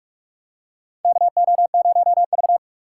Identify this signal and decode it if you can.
Morse code, heard as a keyed tone, about 44 wpm, 690 Hz. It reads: KO0V
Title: USA Call Signs - 4 Characters - 8x Character Spacing - 45wpm